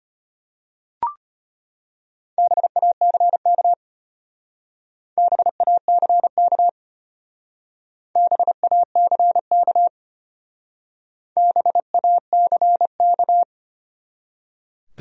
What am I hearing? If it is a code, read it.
Morse code, heard as a keyed tone, about 32 wpm, 700 Hz. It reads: BACK BACK BACK BACK